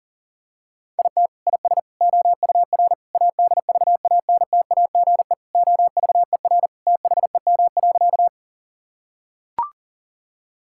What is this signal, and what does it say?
Morse code, heard as a keyed tone, about 40 wpm, 700 Hz. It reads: IT IS OUR ADVANTAGE OVER THEM.
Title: Sentences from Top 1000 Words - Review Part 6 - 40wpm